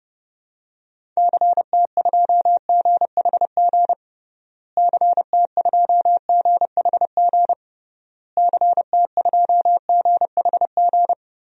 Morse code, heard as a keyed tone, about 32 wpm, 700 Hz. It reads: CT2GHG CT2GHG CT2GHG
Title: DXCC-100 Call Signs - 6 Characters - Repeated 3x - 30wpm